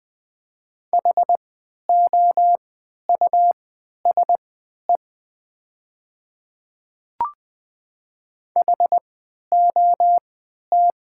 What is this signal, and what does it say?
Morse code, heard as a keyed tone, about 20 wpm, spaced as for 11 wpm, 700 Hz. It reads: HOUSE HOT
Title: 3x Character Spacing - Top 300 Words - 20wpm